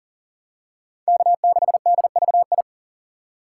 Morse code, heard as a keyed tone, about 40 wpm, 700 Hz. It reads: K6DVI
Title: USA Call Signs - 5 Characters - 2x Character Spacing - 40wpm